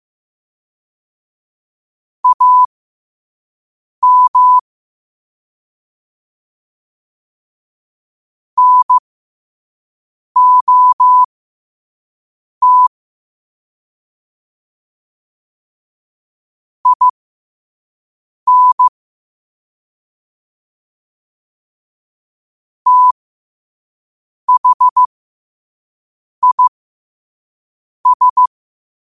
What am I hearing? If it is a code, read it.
Morse code, heard as a keyed tone, about 15 wpm, spaced as for 5 wpm, 1000 Hz. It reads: AM NOT IN THIS
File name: qotdfiles / 5wpm